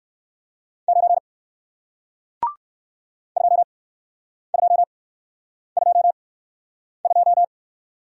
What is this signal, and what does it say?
Morse code, heard as a keyed tone, about 55 wpm, spaced as for 9 wpm, 700 Hz. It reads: 9 2222
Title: Speed-Racing Single Letter-Number 45wpm